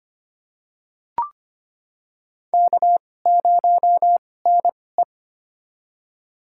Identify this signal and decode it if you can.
Morse code, heard as a keyed tone, about 25 wpm, spaced as for 18 wpm, 700 Hz. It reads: K0NE
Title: USA Call Signs - 2x Character Spacing - 25wpm